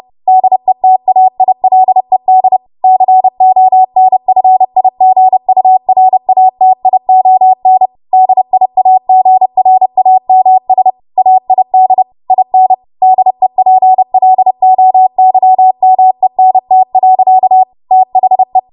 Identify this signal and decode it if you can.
Morse code, heard as a keyed tone, about 30 wpm, 750 Hz. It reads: DETAILED CONFIGURATION DIAGRAMS AID IN DEPLOYMENT. THE
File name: Morse / 30WPM